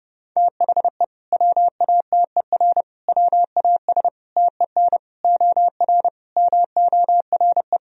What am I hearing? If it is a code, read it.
Morse code, heard as a keyed tone, about 30 wpm, 700 Hz. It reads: THE WATER WAS TEN OR MORE
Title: Sentences from Top 1000 Words - Review Part 3 - 30wpm